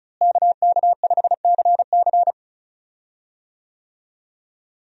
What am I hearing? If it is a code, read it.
Morse code, heard as a keyed tone, about 36 wpm, 700 Hz. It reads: KK5CC
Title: USA Call Signs - 5 Characters - Repeated 3x - 35wpm